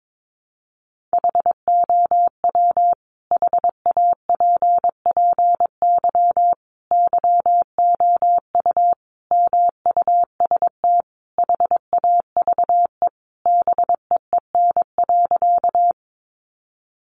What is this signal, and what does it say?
Morse code, heard as a keyed tone, about 22 wpm, 700 Hz. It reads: HOW HAPPY YOU MUST HAVE BEEN.
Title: Sentences from Top 1000 Words 22wpm